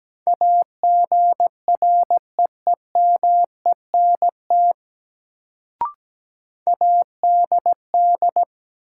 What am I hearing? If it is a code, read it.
Morse code, heard as a keyed tone, about 17 wpm, 700 Hz. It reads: AGREEMENT ADD
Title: Sentences from Top 1500 Words - New Word Drill Alt - 17wpm